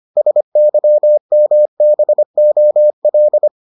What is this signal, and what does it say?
Morse code, heard as a keyed tone, about 25 wpm, 600 Hz. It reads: SYMBOL